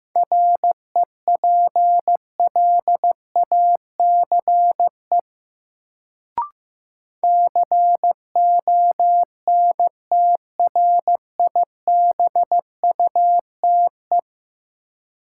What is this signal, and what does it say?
Morse code, heard as a keyed tone, about 15 wpm, 700 Hz. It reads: REPLACE CONTRIBUTE
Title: Sentences from Top 1400 Words - New Word Drill Alt - 15wpm